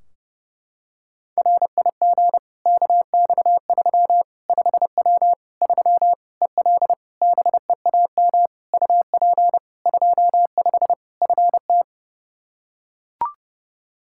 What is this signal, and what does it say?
Morse code, heard as a keyed tone, about 30 wpm, 700 Hz. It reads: RIG KX3 5W 3 EL BEAM UP 25 FT
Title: K1USN SST QSO Practice 30wpm